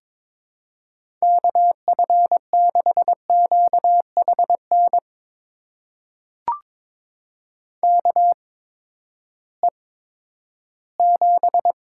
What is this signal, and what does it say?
Morse code, heard as a keyed tone, about 22 wpm, spaced as for 13 wpm, 700 Hz. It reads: KF6QHN KE7
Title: USA Call Signs - 6 Characters - 8x Character Spacing - 22wpm